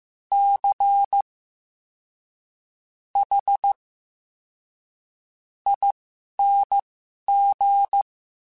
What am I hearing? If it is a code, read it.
Morse code, heard as a keyed tone, about 15 wpm, spaced as for 11 wpm, 800 Hz. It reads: C H ING
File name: Special INGs-mp3s / Special INGs-mp3s / SlowToFaster ing @10